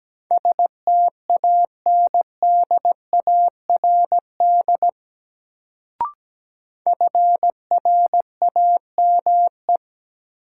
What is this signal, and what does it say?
Morse code, heard as a keyed tone, about 17 wpm, 700 Hz. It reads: STANDARD FRAME
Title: Sentences from Top 1000 Words - Review Part 6 - New Word Drill - 17wpm